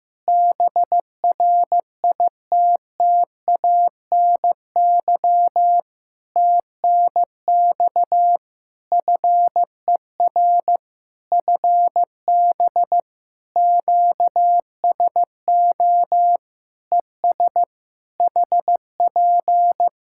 Morse code, heard as a keyed tone, about 15 wpm, 700 Hz. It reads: BRITTANY TNX FER FB QSO ES HP
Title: K1USN SST QSO Practice 15wpm